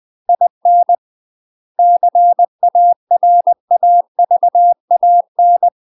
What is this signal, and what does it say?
Morse code, heard as a keyed tone, about 20 wpm, 700 Hz. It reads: IN CARAVAN
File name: UK MASTER QSOs TRAINING / 20WPM / UK MASTER QSOs 0013